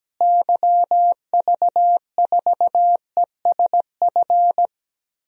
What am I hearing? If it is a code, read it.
Morse code, heard as a keyed tone, about 17 wpm, 700 Hz. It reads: YV4ESF
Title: DXCC-100 Call Signs - 6 Characters - Repeated 3x - 17wpm